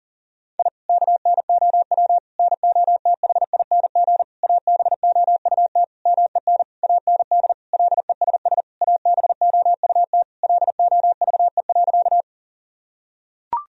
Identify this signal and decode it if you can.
Morse code, heard as a keyed tone, about 40 wpm, 700 Hz. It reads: I KNOW NOTHING ABOUT MEN AND LESS ABOUT LOVE.
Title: Sentences from Top 1000 Words - Review Part 2 - 40wpm